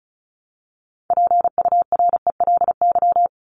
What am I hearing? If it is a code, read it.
Morse code, heard as a keyed tone, about 34 wpm, 700 Hz. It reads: PURELY
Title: LY Suffix - Top 100 Words with Base Words 35wpm